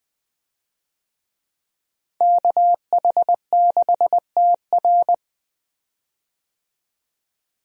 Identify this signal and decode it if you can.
Morse code, heard as a keyed tone, about 21 wpm, 700 Hz. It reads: KH6TR
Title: DXCC-100 Call Signs - 5 Characters - 20wpm